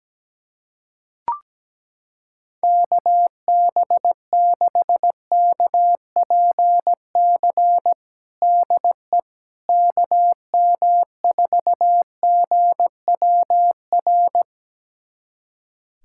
Morse code, heard as a keyed tone, about 17 wpm, 700 Hz. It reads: KB6KPC DE KM4GWR
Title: USA Callsign DE Callsign - 6 Characters - 17wpm